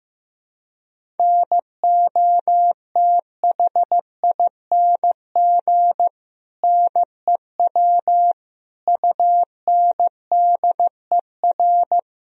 Morse code, heard as a keyed tone, about 15 wpm, 700 Hz. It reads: NOTHING NEW UNDER
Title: Sentences from Top 1000 Words - Review Part 1 - 15wpm